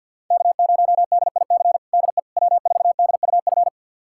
Morse code, heard as a keyed tone, about 52 wpm, 700 Hz. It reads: K0DIC DE W4DUF
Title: USA Callsign DE Callsign - 5 Characters - 50wpm